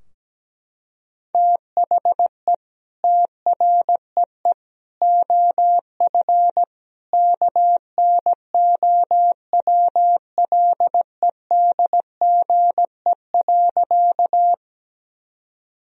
Morse code, heard as a keyed tone, about 17 wpm, 700 Hz. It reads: THE TREE OF KNOWLEDGE.
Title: Sentences from Top 1000 Words - Review Part 3 - 17wpm